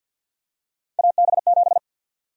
Morse code, heard as a keyed tone, about 51 wpm, 700 Hz. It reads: AB7
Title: USA Call Sign Prefixes - Repeated 2x - 50wpm